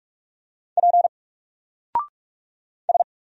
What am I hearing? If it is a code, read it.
Morse code, heard as a keyed tone, about 46 wpm, 700 Hz. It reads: P S
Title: Single Character (Letter:Number:Punctuation:Prosign) 45wpm